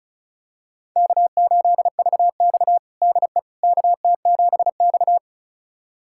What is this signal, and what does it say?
Morse code, heard as a keyed tone, about 34 wpm, 700 Hz. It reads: K8VX DE KT7X
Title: USA Callsign DE Callsign - 4 Characters - 35wpm